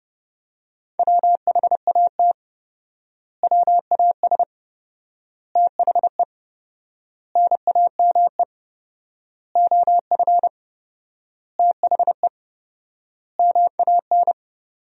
Morse code, heard as a keyed tone, about 32 wpm, 700 Hz. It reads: WHAT WAS THE NAME OF THE MAN
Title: 4x Word-Spacing - Sentences from Top 300 Words - 30wpm